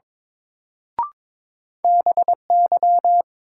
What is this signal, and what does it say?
Morse code, heard as a keyed tone, about 22 wpm, 700 Hz. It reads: BY
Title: Sentences from Top 100 Words - New Word Drill - 22wpm